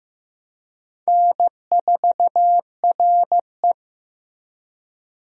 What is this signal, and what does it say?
Morse code, heard as a keyed tone, about 16 wpm, 700 Hz. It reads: N4RE
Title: USA Call Signs - 4 Characters - Repeated 3x - 15wpm